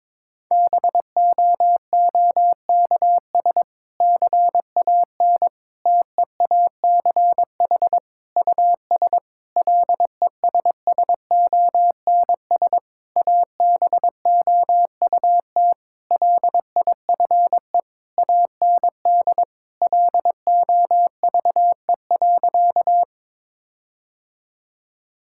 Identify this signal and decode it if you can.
Morse code, heard as a keyed tone, about 22 wpm, 700 Hz. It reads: BOOKS CAN TEACH US LESSONS ABOUT LIFE AND LOVE.
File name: Difficult Sentences with Hints - 22wpm